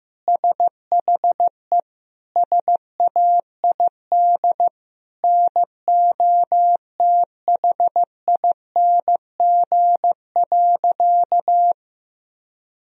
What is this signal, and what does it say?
Morse code, heard as a keyed tone, about 15 wpm, 700 Hz. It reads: SHE SAID NOTHING.
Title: Sentences from Top 1000 Words - Review Part 2 - 15wpm